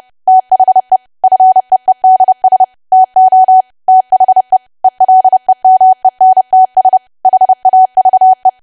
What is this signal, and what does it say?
Morse code, heard as a keyed tone, about 30 wpm, 750 Hz. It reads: THE FEEDS TO THE ELEMENTS HAVE
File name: cwa / W1AW-Level3 / 30WPM